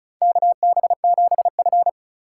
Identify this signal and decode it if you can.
Morse code, heard as a keyed tone, about 36 wpm, 700 Hz. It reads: KB7F
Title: USA Call Signs - 4 Characters - 2x Character Spacing - 35wpm